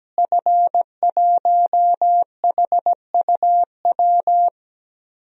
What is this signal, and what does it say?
Morse code, heard as a keyed tone, about 17 wpm, 700 Hz. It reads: F1HUW
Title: DXCC-100 Call Signs - 5 Characters - Repeated 3x - 17wpm